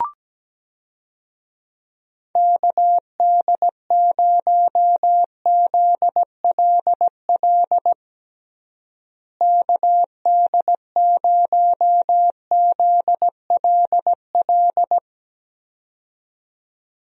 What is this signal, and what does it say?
Morse code, heard as a keyed tone, about 17 wpm, 700 Hz. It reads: KD0ZLL KD0ZLL
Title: USA Call Signs - 6 Characters - Repeated 2x - 17wpm